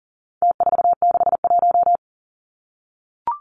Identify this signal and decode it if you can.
Morse code, heard as a keyed tone, about 40 wpm, 700 Hz. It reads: T461
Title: Numbers - Contest Style - 4-Digit 40wpm